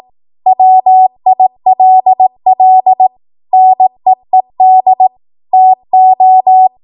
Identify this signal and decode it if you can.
Morse code, heard as a keyed tone, about 18 wpm, 750 Hz. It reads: WILL NEED TO
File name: Archive / 18 WPM / 18WPM